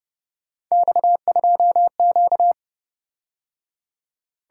Morse code, heard as a keyed tone, about 29 wpm, 700 Hz. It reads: X2Q
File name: Speed-Racing - Three Letter-Number Combinations - 25wpm